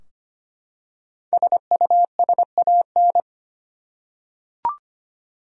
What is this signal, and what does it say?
Morse code, heard as a keyed tone, about 25 wpm, 700 Hz. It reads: SUSAN
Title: USA Common Names - Max 5 Letters - 25wpm